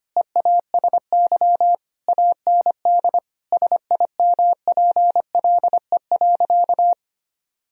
Morse code, heard as a keyed tone, about 25 wpm, 700 Hz. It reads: EASY AND SIMPLE.
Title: Sentences from Top 1000 Words - Review Part 3 - 25wpm